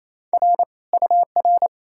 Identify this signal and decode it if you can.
Morse code, heard as a keyed tone, about 37 wpm, spaced as for 33 wpm, 700 Hz. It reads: R UR
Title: Speed-Racing Common QSO Phrases - 28wpm